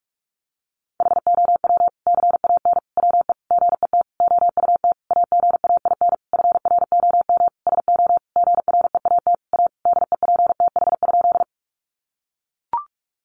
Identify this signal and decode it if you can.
Morse code, heard as a keyed tone, about 44 wpm, 700 Hz. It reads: HOW CAN WE GET OUT AGAIN FROM SO GREAT A DEPTH?
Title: Sentences from Top 1200 Words 45wpm